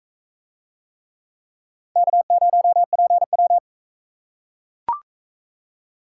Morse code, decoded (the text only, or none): K0PW